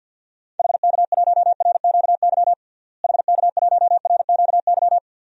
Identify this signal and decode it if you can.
Morse code, heard as a keyed tone, about 53 wpm, 700 Hz. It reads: HK1RQY HK1RQY
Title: DXCC-100 Call Signs - 6 Characters - Repeated 2x - 50wpm